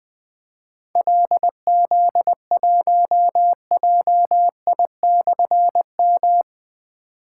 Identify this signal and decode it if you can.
Morse code, heard as a keyed tone, about 20 wpm, 700 Hz. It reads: LZ1JI/M
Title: Difficult Call Signs 20wpm